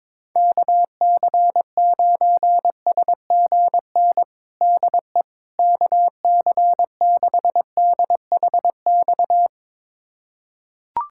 Morse code, heard as a keyed tone, about 22 wpm, 700 Hz. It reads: KC9SGN DE KC6DHX
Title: USA Callsign DE Callsign 22wpm